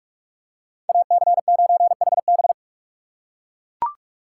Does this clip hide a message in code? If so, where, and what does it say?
Yes: Morse code from 0.9 to 2.5 s, heard as a keyed tone, about 39 wpm, 700 Hz. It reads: AC9HB